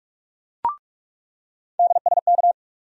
Morse code, heard as a keyed tone, about 47 wpm, 700 Hz. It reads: DSK
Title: USA Call Sign Postfixes - 3 Letters 45wpm